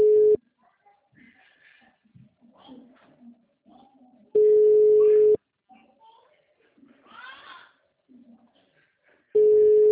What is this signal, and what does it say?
Ringback tone (Europe, 425 Hz), with speech.
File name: out- -2001- - - .291